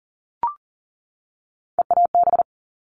Morse code, heard as a keyed tone, about 40 wpm, 700 Hz. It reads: EAB
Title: All Cut Numbers - 3-Digit 40wpm